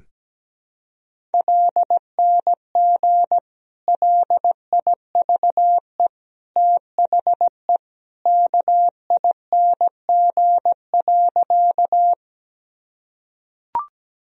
Morse code, heard as a keyed tone, about 17 wpm, 700 Hz. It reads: LNG LIVE THE KING.
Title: Sentences from Top 400 Words and Abbreviations 17wpm